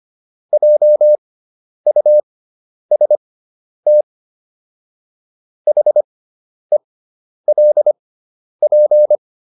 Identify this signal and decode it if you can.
Morse code, heard as a keyed tone, about 25 wpm, spaced as for 10 wpm, 600 Hz. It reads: JUST HELP